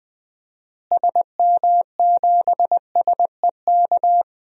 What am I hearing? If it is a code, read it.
Morse code, heard as a keyed tone, about 20 wpm, 700 Hz. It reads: SM7SEK
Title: DXCC-100 Call Signs - 6 Characters - Repeated 3x - 20wpm